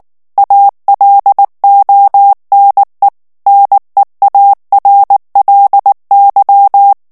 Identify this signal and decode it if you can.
Morse code, heard as a keyed tone, about 19 wpm, 800 Hz. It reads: ALONE NEARLY
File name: ANGLAIS texte clair 2 vitesse 18 mots minute